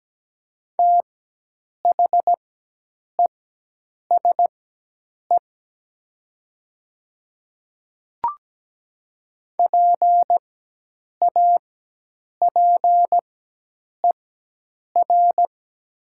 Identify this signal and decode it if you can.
Morse code, heard as a keyed tone, about 17 wpm, spaced as for 8 wpm, 700 Hz. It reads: THESE PAPER
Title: 4x Character Spacing - Top 300 Words - 17wpm